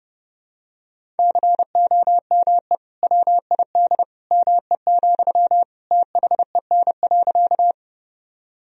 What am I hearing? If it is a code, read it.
Morse code, heard as a keyed tone, about 32 wpm, 700 Hz. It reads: COME WID ME, THEN.
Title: Sentences from Top 200 Words and Abbreviations 30wpm